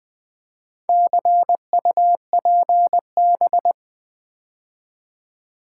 Morse code, heard as a keyed tone, about 20 wpm, 700 Hz. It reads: CUPB
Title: Build A Word - 4 Letters At A Time - 8-Letter Words - 20wpm